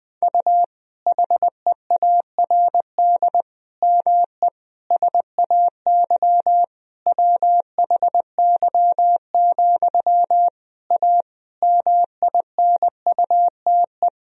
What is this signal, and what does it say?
Morse code, heard as a keyed tone, about 20 wpm, 700 Hz. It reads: U HEARD ME SAY WHY, A MINUTE
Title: Sentences from Top 500 Words and Abbreviations 20wpm